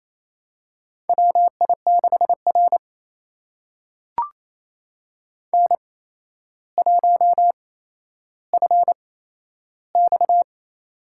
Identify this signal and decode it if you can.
Morse code, heard as a keyed tone, about 28 wpm, spaced as for 12 wpm, 700 Hz. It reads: WI6R N1FX
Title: USA Call Signs - 4 Characters - 8x Character Spacing - 28wpm